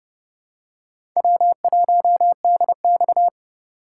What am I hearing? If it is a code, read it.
Morse code, heard as a keyed tone, about 30 wpm, 700 Hz. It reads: W1DX
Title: K1USN SST QSO Practice - 30wpm